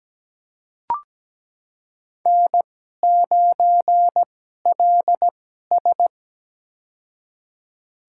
Morse code, heard as a keyed tone, about 17 wpm, spaced as for 12 wpm, 700 Hz. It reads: N9LS
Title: USA Call Signs - 4 Characters - 2x Character Spacing - 17wpm